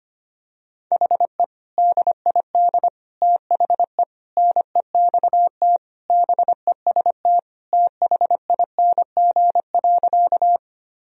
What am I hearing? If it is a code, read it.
Morse code, heard as a keyed tone, about 25 wpm, 700 Hz. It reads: HE DID THE NEXT BEST THING.
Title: Sentences from Top 1000 Words - Review Part 1 - 25wpm